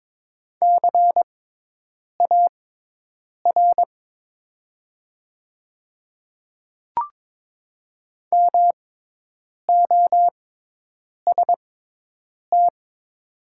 Morse code, heard as a keyed tone, about 22 wpm, spaced as for 8 wpm, 700 Hz. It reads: CAR MOST
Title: 6x Character Spacing - Top 500 Words - 22wpm